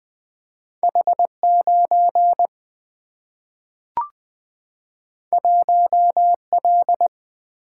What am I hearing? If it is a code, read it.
Morse code, heard as a keyed tone, about 20 wpm, 700 Hz. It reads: H9 1L
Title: Single-Letter Single-Number 20wpm